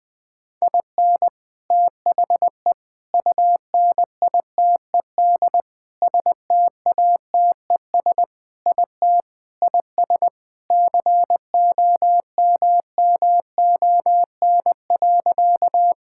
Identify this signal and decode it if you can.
Morse code, heard as a keyed tone, about 20 wpm, 700 Hz. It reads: IN THE UNITED STATES IT IS COMMON.
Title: Sentences with 4-Word Phrases - 20wpm